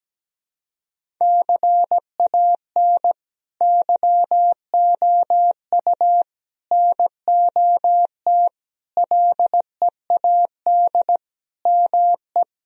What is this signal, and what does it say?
Morse code, heard as a keyed tone, about 17 wpm, 700 Hz. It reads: CAN YOU NOT LEAD ME
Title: Sentences from Top 1000 Words - Review Part 2 - 17wpm